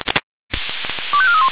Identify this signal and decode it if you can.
Five-tone selective call (ZVEI), heard as quick runs of five tones, tones of about 70 ms. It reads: 25431
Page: ZVEI